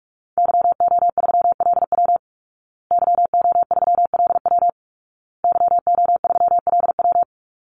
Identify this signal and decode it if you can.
Morse code, heard as a keyed tone, about 44 wpm, 700 Hz. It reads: YO3LW YO3LW YO3LW
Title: Call Signs Repeated 2x - 45wpm